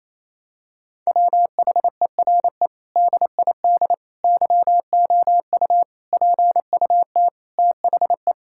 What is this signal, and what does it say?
Morse code, heard as a keyed tone, about 27 wpm, 690 Hz. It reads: WHERE DID YOU PUT THE
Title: Sentences from Top 1000 Words - Review Part 6 - 28wpm